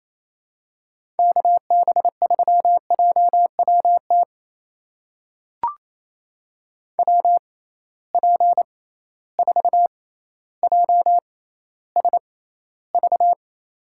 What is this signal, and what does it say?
Morse code, heard as a keyed tone, about 28 wpm, spaced as for 14 wpm, 700 Hz. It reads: KB3JWT WP4JSV
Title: USA Call Signs - 6 Characters - 6x Character Spacing - 28wpm